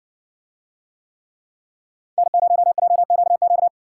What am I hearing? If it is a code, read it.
Morse code, heard as a keyed tone, about 55 wpm, 700 Hz. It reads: N0JQQ